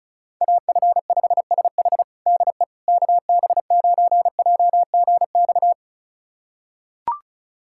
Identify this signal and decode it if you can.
Morse code, heard as a keyed tone, about 36 wpm, 700 Hz. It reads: AF5SH DE KB9JGX
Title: USA Callsign DE Callsign 35wpm